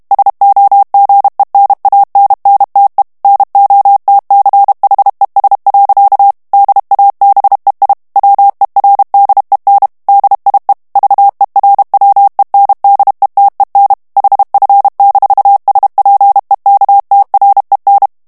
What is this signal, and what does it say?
Morse code, heard as a keyed tone, about 32 wpm, 800 Hz. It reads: SOGENANNTE NOTCHES. DABEI WERDEN DIE VERWENDETEN HF-SPEKTREN
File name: ALLEMAND texte clair 1 vitesse 30 mots minute